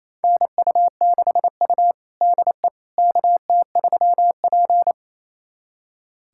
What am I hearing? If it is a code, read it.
Morse code, heard as a keyed tone, about 28 wpm, 700 Hz. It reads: NU6U DE KT3P
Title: USA Callsign DE Callsign - 4 Characters - 28wpm